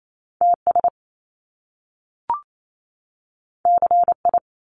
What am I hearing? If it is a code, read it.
Morse code, heard as a keyed tone, about 28 wpm, 700 Hz. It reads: TS CI
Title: Top Two Letter Combinations in Words 28wpm